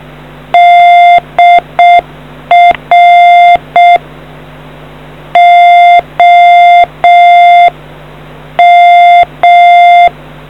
Morse code, heard as a keyed tone, about 6 wpm, 700 Hz. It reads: DR OM